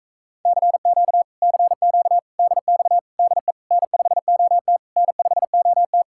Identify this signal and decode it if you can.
Morse code, heard as a keyed tone, about 37 wpm, 700 Hz. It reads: CQ CQ DX DE N5OT N5OT